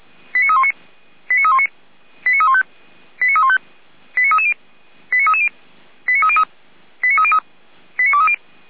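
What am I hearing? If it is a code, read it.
Five-tone selective call (ZVEI), heard as quick runs of five tones, tones of about 70 ms. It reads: 89319 89319 89315 89315 89330 89330 89303 89303 89230
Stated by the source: ZVEI five-tone selective call 89125 received by SDR.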